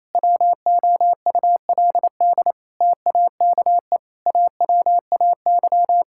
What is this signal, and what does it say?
Morse code, heard as a keyed tone, about 28 wpm, 700 Hz. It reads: WOULD TAKE AWAY
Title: Sentences from Top 1600 Words 28wpm